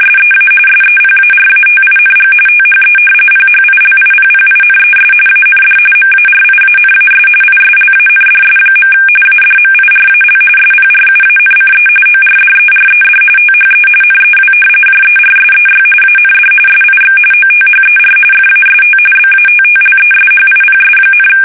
RTTY, about 45.45 baud. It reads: SEA SEPARATING THE PHILIPPINES AND MARIANAS ISLANDS. ADMIRAL SPRUANCE ORDERED THE NEW JERSEY AND SIX OTHER FAST BATTLESHIPS,
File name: fsk850 / battleshipnj